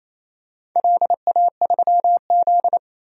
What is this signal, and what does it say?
Morse code, heard as a keyed tone, about 28 wpm, 700 Hz. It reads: LA3Z
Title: DXCC-100 Call Signs - 4 Characters - 28wpm